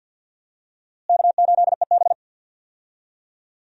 Morse code, heard as a keyed tone, about 52 wpm, 700 Hz. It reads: K8EB
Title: USA Call Signs - 4 Characters - Repeated 2x - 50wpm